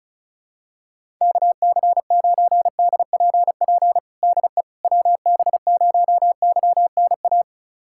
Morse code, heard as a keyed tone, about 34 wpm, 700 Hz. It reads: KC9DPP DE WB0YNA
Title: USA Callsign DE Callsign - 6 Characters - 35wpm